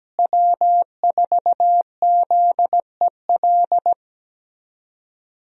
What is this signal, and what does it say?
Morse code, heard as a keyed tone, about 17 wpm, 700 Hz. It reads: W4ZEL
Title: USA Call Signs - 5 Characters - Repeated 2x - 17wpm